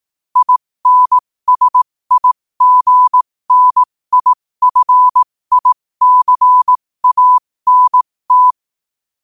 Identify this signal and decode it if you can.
Morse code, heard as a keyed tone, about 18 wpm, spaced as for 15 wpm, 1000 Hz. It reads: INSIGNIFICANT
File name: qotdfiles / 15wpm